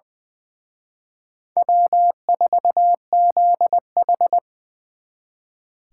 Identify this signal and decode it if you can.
Morse code, heard as a keyed tone, about 23 wpm, 700 Hz. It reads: W4ZH
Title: Speed Racing - USA 1x2 Call Signs 20wpm